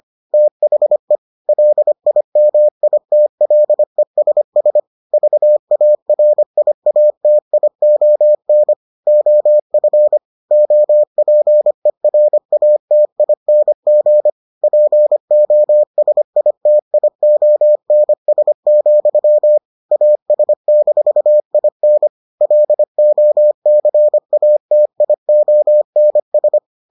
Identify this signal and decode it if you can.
Morse code, heard as a keyed tone, about 25 wpm, 600 Hz. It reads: THE LIMITLESS VARIATION OF OPERATING POSITIONS, AS-IN LOCATIONS